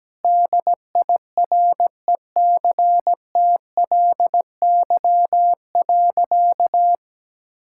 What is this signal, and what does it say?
Morse code, heard as a keyed tone, about 17 wpm, 700 Hz. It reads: DIRECTLY.
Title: Sentences from Top 1000 Words - Review Part 5 - 17wpm